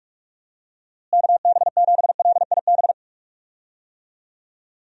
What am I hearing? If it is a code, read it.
Morse code, heard as a keyed tone, about 41 wpm, 700 Hz. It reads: KB7LIB
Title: USA Call Signs - 6 Characters - Repeated 2x - 45wpm